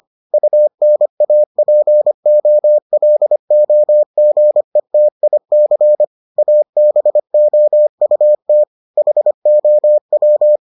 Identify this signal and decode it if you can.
Morse code, heard as a keyed tone, about 25 wpm, 600 Hz. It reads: UNAPOLOGETIC ABOUT HOW